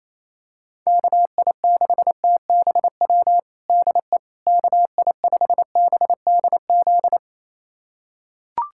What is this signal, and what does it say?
Morse code, heard as a keyed tone, about 28 wpm, 700 Hz. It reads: KI6TBW DE KI5BDZ
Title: USA Callsign DE Callsign - 6 Characters - 28wpm